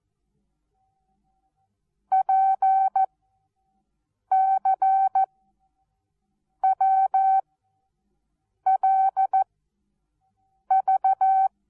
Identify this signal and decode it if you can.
Morse code, heard as a keyed tone, about 14 wpm, 770 Hz. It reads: P C W L V
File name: Onlinelehrgang / morsen / DJ4UF Morsen19